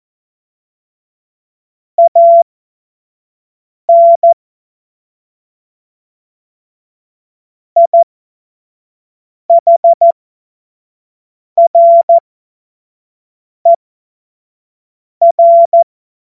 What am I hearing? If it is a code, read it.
Morse code, heard as a keyed tone, about 14 wpm, spaced as for 5 wpm, 680 Hz. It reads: AN IHRER
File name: Morsekurs / Witze / Witz 20 Februar 2024 25 F70